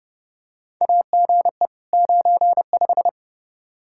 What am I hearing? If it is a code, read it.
Morse code, heard as a keyed tone, about 28 wpm, 700 Hz. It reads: AGE 95
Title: K1USN SST QSO Practice 30wpm